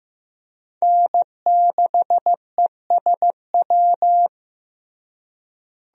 Morse code, heard as a keyed tone, about 16 wpm, 700 Hz. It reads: N6ESW
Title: USA Call Signs - 5 Characters - Repeated 2x - 15wpm